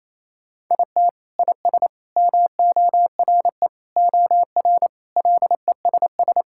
Morse code, heard as a keyed tone, about 27 wpm, 700 Hz. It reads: IT IS MORE OR LESS
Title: Sentences from Top 1000 Words - Review Part 6 - 28wpm